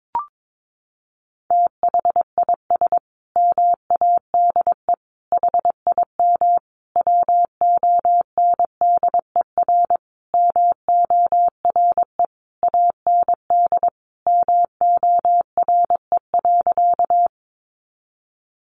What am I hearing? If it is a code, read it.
Morse code, heard as a keyed tone, about 22 wpm, 700 Hz. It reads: THIS MADE HIM WONDER MORE AND MORE.
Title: Sentences from Top 500 Words 22wpm